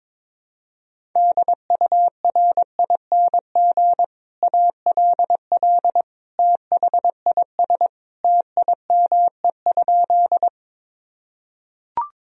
Morse code, heard as a keyed tone, about 21 wpm, 700 Hz. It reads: DURING ALL THIS TIME?
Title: Sentences from Top 1000 Words - Review Part 2 - 22wpm